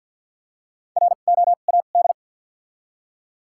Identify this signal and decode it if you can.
Morse code, heard as a keyed tone, about 53 wpm, spaced as for 37 wpm, 700 Hz. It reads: ROAD
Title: 2x Character Spacing - Top 500 Words - 50wpm